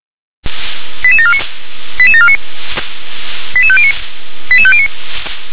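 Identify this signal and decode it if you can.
Five-tone selective call (ZVEI), heard as quick runs of five tones, tones of about 70 ms. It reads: 88749 88749 88509 88509
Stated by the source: ZVEI five-tone call 88853 received on an SDR.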